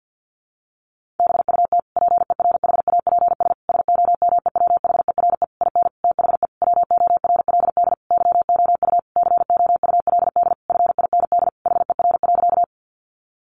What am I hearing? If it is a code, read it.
Morse code, heard as a keyed tone, about 50 wpm, 700 Hz. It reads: BUT PERHAPS SOMEWHERE IN THE WORLD YOU COULD FIND HER.
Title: Sentences from Top 1400 Words 50wpm